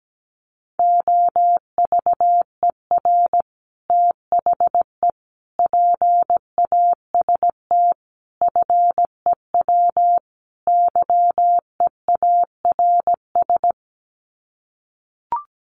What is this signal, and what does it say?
Morse code, heard as a keyed tone, about 17 wpm, 700 Hz. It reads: OVER THE PAST FEW YEARS
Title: Sets of 5 Words Encore 17wpm